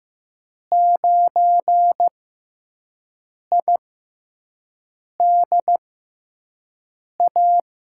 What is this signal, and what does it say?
Morse code, heard as a keyed tone, about 15 wpm, spaced as for 5 wpm, 700 Hz. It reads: 9IDA
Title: USA Call Signs - 5 Characters - 6x Character Spacing - 15wpm